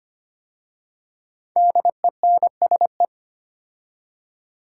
Morse code, heard as a keyed tone, about 25 wpm, 700 Hz. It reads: DENSE
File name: 5 Letter Words - First 3 Letters Given - 25wpm